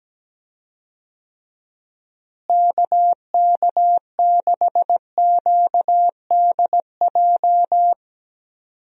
Morse code, heard as a keyed tone, about 17 wpm, 700 Hz. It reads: KK6QDJ
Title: USA Call Signs - 6 Characters - Repeated 3x - 17wpm